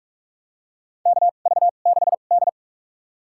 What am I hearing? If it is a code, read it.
Morse code, heard as a keyed tone, about 46 wpm, spaced as for 33 wpm, 700 Hz. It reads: KV6D
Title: USA Call Signs - 2x Character Spacing - 45wpm